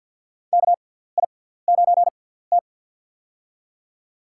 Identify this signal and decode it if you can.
Morse code, heard as a keyed tone, about 47 wpm, spaced as for 17 wpm, 700 Hz. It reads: KI9T
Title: USA Call Signs - 4 Characters - 6x Character Spacing - 50wpm